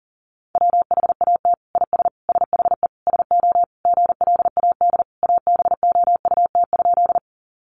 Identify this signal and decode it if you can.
Morse code, heard as a keyed tone, about 40 wpm, 690 Hz. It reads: WHAT IS SHE SO GLAD ABOUT?
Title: Sentences from Top 800 Words 40wpm